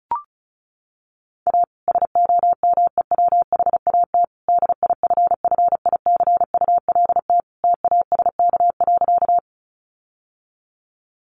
Morse code, heard as a keyed tone, about 34 wpm, 700 Hz. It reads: A SOMEWHAT DIFFICULT TASK.
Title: Sentences from Top 1300 Words 35wpm